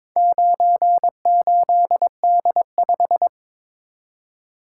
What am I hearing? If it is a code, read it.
Morse code, heard as a keyed tone, about 22 wpm, 700 Hz. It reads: 98D5
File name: practice / 22wpm / Four Letter-Number Combinations - 22wpm